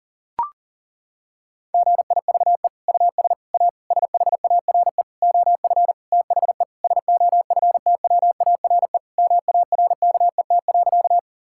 Morse code, heard as a keyed tone, about 40 wpm, 700 Hz. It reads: GIVE US A SHARE OF THE SOFTWARE MARKET.
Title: Sentences from Top 1000 Words - Review Part 6 - 40wpm